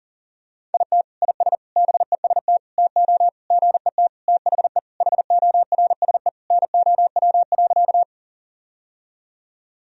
Morse code, heard as a keyed tone, about 40 wpm, 700 Hz. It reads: IT IS BEST TO GET THE HORSE NOW.
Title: Sentences from Top 1000 Words - Review Part 1 - 40wpm